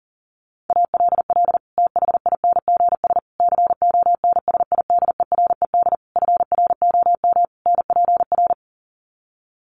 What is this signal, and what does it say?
Morse code, heard as a keyed tone, about 40 wpm, 700 Hz. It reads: ALL THINGS CONSIDERED FROM NPR
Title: Sets of 5 Words Encore 40wpm